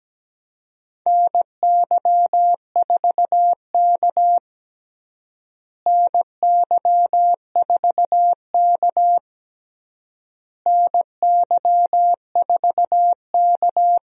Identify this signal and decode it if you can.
Morse code, heard as a keyed tone, about 17 wpm, 700 Hz. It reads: NY4K NY4K NY4K
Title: USA Call Signs - 4 Characters - Repeated 3x - 17wpm